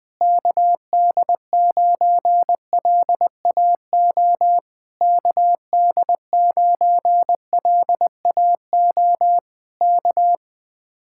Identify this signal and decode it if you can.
Morse code, heard as a keyed tone, about 21 wpm, 700 Hz. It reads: KD9LAO KD9LAO K
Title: Speed-Racing Common QSO Phrases - 20wpm